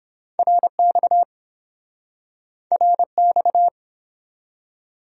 Morse code, heard as a keyed tone, about 31 wpm, 700 Hz. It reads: RX RX